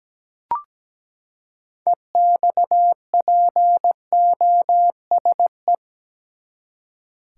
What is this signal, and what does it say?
Morse code, heard as a keyed tone, about 17 wpm, 700 Hz. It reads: EXPOSE
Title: Six Letter Words - 17wpm